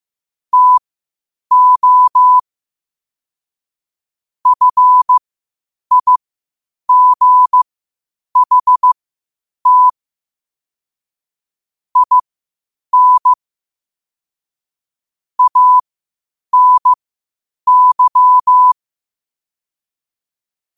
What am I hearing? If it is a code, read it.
Morse code, heard as a keyed tone, about 15 wpm, spaced as for 8 wpm, 1000 Hz. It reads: TO FIGHT IN ANY
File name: qotdfiles / 8wpm